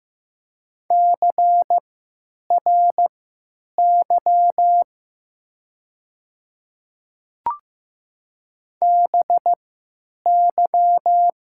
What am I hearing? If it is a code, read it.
Morse code, heard as a keyed tone, about 15 wpm, spaced as for 8 wpm, 700 Hz. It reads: CRY BY
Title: 3x Character Spacing - Top 500 Words - 15wpm